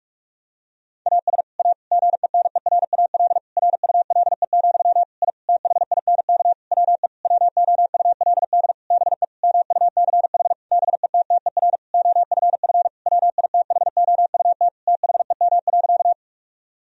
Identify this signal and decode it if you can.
Morse code, heard as a keyed tone, about 41 wpm, 700 Hz. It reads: AS A GENERAL RULE, I THINK WE WOULD BE MUCH BETTER OFF WITHOUT THEM.
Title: Sentences from Top 1000 Words - Review Part 3 - 45wpm